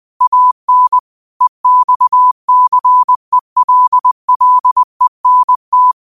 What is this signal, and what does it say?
Morse code, heard as a keyed tone, about 20 wpm, 1000 Hz. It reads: AN EXCELLENT